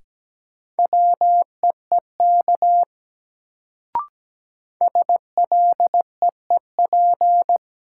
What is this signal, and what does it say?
Morse code, heard as a keyed tone, about 17 wpm, 700 Hz. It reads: WEEK SLEEP
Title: Sentences from Top 500 Words - New Word Drill - 17wpm